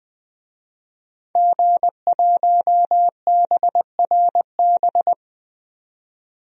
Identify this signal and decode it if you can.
Morse code, heard as a keyed tone, about 20 wpm, 700 Hz. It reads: G1BRB
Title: DXCC-100 Call Signs - 5 Characters - 20wpm